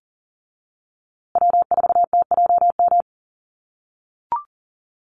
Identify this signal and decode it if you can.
Morse code, heard as a keyed tone, about 40 wpm, 700 Hz. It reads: W4TJM
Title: USA Call Signs Repeated 2x 40wpm